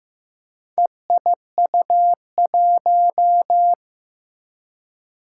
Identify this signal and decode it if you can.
Morse code, heard as a keyed tone, about 15 wpm, 700 Hz. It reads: EIU1
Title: Four Challenging Letter-Number Combinations 15wpm